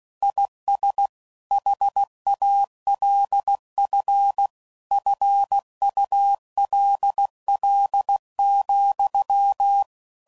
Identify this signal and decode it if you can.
Morse code, heard as a keyed tone, about 16 wpm, 800 Hz. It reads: IS HALF FULL,